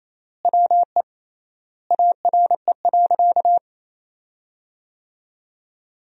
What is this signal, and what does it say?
Morse code, heard as a keyed tone, about 28 wpm, 700 Hz. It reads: WE ARE.
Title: 3x Word-Spacing - Sentences from Top 200 Words - 28wpm